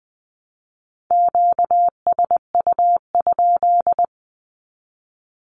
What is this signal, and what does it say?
Morse code, heard as a keyed tone, about 20 wpm, 700 Hz. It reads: QSU?
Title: Q-Codes 20wpm